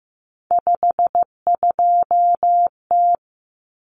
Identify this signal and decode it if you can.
Morse code, heard as a keyed tone, about 15 wpm, 700 Hz. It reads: 52T
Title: Common Cut Numbers - 3-Digit 15wpm